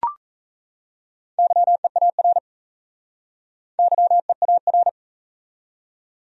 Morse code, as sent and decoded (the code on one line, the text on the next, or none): -.-- . .- .-. / -.-- . .- .-.
YEAR YEAR